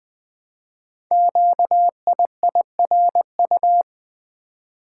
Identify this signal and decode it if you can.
Morse code, heard as a keyed tone, about 20 wpm, 700 Hz. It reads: QIIRU
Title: Code Groups - 5 Letters - 20wpm